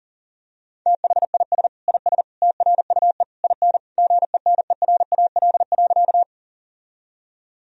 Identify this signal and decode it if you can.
Morse code, heard as a keyed tone, about 40 wpm, 700 Hz. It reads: THIS IS TRUE IN GENERAL.
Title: Sentences from Top 1000 Words - Review Part 3 - 40wpm